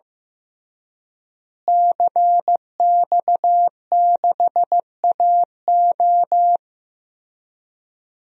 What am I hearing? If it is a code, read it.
Morse code, heard as a keyed tone, about 15 wpm, 700 Hz. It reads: CX6AO
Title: DXCC-100 Call Signs - 5 Characters - Repeated 3x - 15wpm